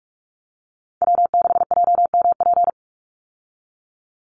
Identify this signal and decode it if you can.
Morse code, heard as a keyed tone, about 44 wpm, 700 Hz. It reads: W6JMP
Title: Call Signs - 45wpm